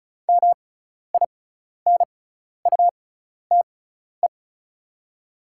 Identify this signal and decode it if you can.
Morse code, heard as a keyed tone, about 36 wpm, spaced as for 12 wpm, 690 Hz. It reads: MINUTE
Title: 6x Character Spacing - Top 500 Words - 35wpm